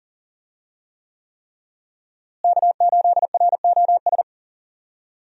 Morse code, heard as a keyed tone, about 40 wpm, 700 Hz. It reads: K8ROS
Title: USA Call Signs - 5 Characters - Repeated 3x - 40wpm